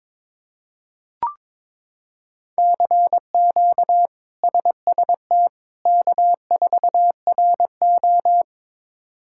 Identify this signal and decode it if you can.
Morse code, heard as a keyed tone, about 22 wpm, 700 Hz. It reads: CQ SST K4RO
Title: K1USN SST QSO Practice - 22wpm